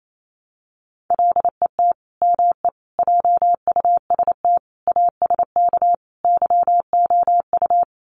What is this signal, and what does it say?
Morse code, heard as a keyed tone, about 28 wpm, 700 Hz. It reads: LET ME JUST ASK YOU
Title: Sets of 5 Words Encore2 28wpm